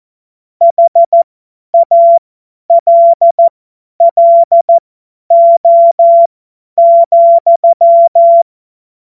Morse code, heard as a keyed tone, about 14 wpm, spaced as for 10 wpm, 680 Hz. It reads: HALLO,
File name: Morsekurs / Witze / Witz 17. Mai 2022 50 F70